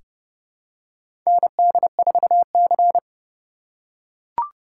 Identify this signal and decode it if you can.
Morse code, heard as a keyed tone, about 31 wpm, 700 Hz. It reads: ND4C
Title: USA Call Signs - 4 Characters - 2x Character Spacing - 30wpm